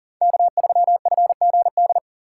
Morse code, heard as a keyed tone, about 40 wpm, 700 Hz. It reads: K3FGD
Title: USA Call Signs - 2x Character Spacing - 40wpm